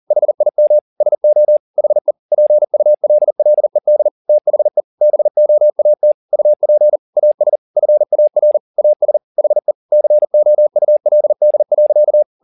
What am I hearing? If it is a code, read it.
Morse code, heard as a keyed tone, about 40 wpm, 600 Hz. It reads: HIM SO HE PULLED THE BOAT UP AS FAR AS HE COULD.
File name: the old man and the sea 2 40wpm